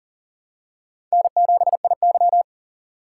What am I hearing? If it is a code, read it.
Morse code, heard as a keyed tone, about 38 wpm, 700 Hz. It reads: N7IY
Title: USA Call Signs - 4 Characters - 6x Character Spacing - 40wpm